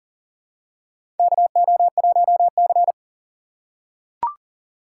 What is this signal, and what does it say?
Morse code, heard as a keyed tone, about 40 wpm, 700 Hz. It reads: KO1C
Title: USA Call Signs - 4x Character Spacing - 40wpm